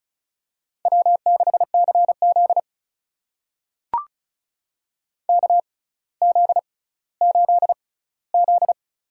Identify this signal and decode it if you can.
Morse code, heard as a keyed tone, about 36 wpm, spaced as for 18 wpm, 700 Hz. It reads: W6CZ KZ8Z
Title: USA Call Signs - 6x Character Spacing - 35wpm